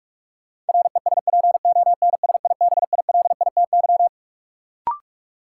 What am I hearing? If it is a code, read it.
Morse code, heard as a keyed tone, about 46 wpm, 700 Hz. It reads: RESPONSIBILITY
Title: Sentences from Top 1200 Words - New Word Drill - 45wpm